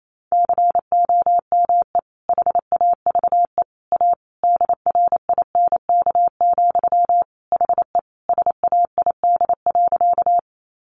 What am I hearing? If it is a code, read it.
Morse code, heard as a keyed tone, about 28 wpm, 700 Hz. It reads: COME HAVE A DRINK, HE SAID.
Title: Sentences from Top 1500 Words 28wpm